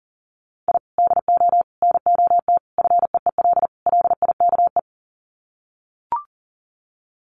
Morse code, heard as a keyed tone, about 37 wpm, 700 Hz. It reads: I DO NOT FEEL LIKE
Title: Sets of 5 Words Encore 40wpm